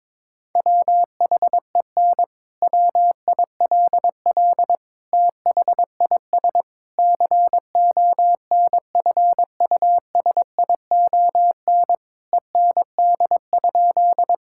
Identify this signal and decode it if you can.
Morse code, heard as a keyed tone, about 22 wpm, 700 Hz. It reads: WHEN WILL THIS CONFUSION END?
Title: Sentences from Top 1600 Words 22wpm